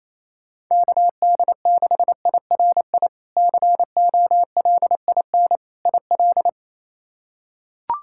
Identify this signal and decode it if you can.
Morse code, heard as a keyed tone, about 28 wpm, 700 Hz. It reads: KD6IRI COLIN IL
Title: K1USN SST QSO Practice - 28wpm